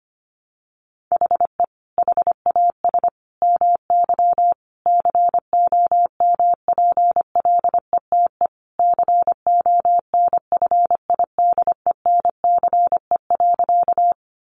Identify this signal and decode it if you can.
Morse code, heard as a keyed tone, about 25 wpm, 700 Hz. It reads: HE HAS MY COMPLETE CONFIDENCE.
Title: Sentences from Top 1500 Words 25wpm